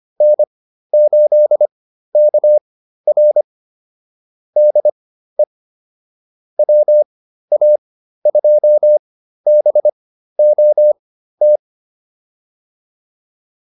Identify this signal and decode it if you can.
Morse code, heard as a keyed tone, about 25 wpm, spaced as for 13 wpm, 600 Hz. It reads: N8KR DE WA2BOT